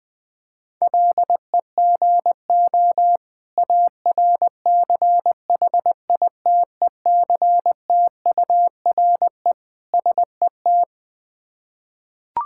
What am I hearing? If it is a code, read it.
Morse code, heard as a keyed tone, about 20 wpm, 700 Hz. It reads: LEGO ARCHITECTURE SET
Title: Difficult Words with Hints - 20wpm